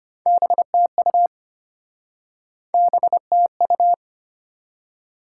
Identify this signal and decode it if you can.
Morse code, heard as a keyed tone, about 33 wpm, 700 Hz. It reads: BTU BTU